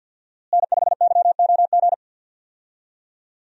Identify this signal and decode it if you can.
Morse code, heard as a keyed tone, about 48 wpm, 700 Hz. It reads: N5YOG
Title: USA Call Signs - 5 Characters - Repeated 2x - 50wpm